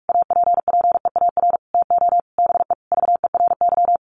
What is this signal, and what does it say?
Morse code, heard as a keyed tone, about 44 wpm, 700 Hz. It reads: APPEAR TO BE VERY
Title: Sentences from Top 1200 Words 45wpm